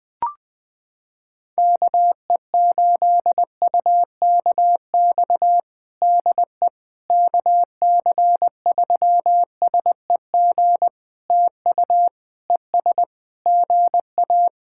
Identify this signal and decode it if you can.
Morse code, heard as a keyed tone, about 20 wpm, 700 Hz. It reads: KE8UKX DE KC3SEG TU ES GA
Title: POTA QSO Practice - 20wpm